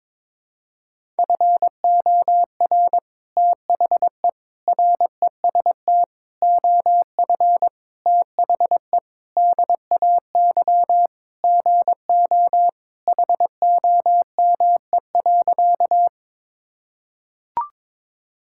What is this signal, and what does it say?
Morse code, heard as a keyed tone, about 22 wpm, 700 Hz. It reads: FOR THE REST OF THE DAY GO HOME.
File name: practice / 22wpm / Sentences with 3-Word Phrases - 22wpm